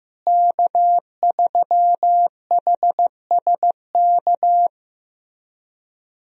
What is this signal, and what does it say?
Morse code, heard as a keyed tone, about 16 wpm, 700 Hz. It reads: K3HSK
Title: USA Call Signs - 5 Characters - Repeated 2x - 15wpm